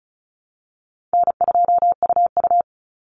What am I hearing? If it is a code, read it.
Morse code, heard as a keyed tone, about 34 wpm, 700 Hz. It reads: N2UU
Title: Call Signs Repeated 2x - 35wpm